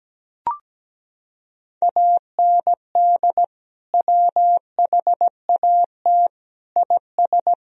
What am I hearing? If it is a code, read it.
Morse code, heard as a keyed tone, about 17 wpm, 700 Hz. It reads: AND WHAT IS
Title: Sentences from Top 1000 Words - Review Part 3 - 17wpm